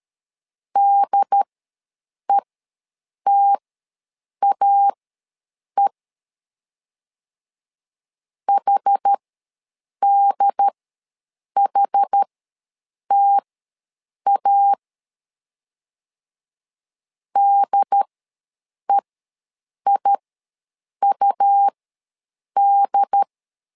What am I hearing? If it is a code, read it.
Morse code, heard as a keyed tone, about 13 wpm, spaced as for 7 wpm, 800 Hz. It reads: DETAE HDHTA DEIUD
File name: LEZIONI / 2 UHDN / 2 UHD ws 12wpm 1722